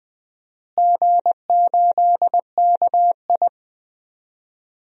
Morse code, heard as a keyed tone, about 21 wpm, 700 Hz. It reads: G8KI
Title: DXCC-100 Call Signs - 4 Characters - Repeated 2x - 20wpm